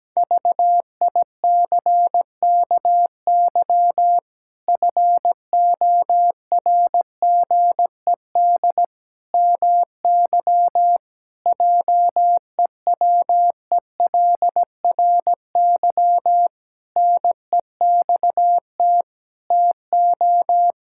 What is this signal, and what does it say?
Morse code, heard as a keyed tone, about 17 wpm, 700 Hz. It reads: VICKY FORGED MY JEWELRY NEXT TO
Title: Pangram Challenge v2 - 17wpm